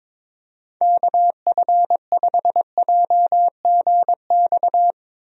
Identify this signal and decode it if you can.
Morse code, heard as a keyed tone, about 22 wpm, 700 Hz. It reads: KF5JGX
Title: USA Call Signs - 6 Characters - 22wpm